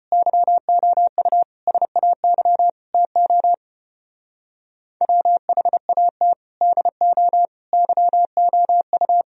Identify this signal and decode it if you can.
Morse code, heard as a keyed tone, about 32 wpm, 700 Hz. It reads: YOU SAY TO WHAT DO YOU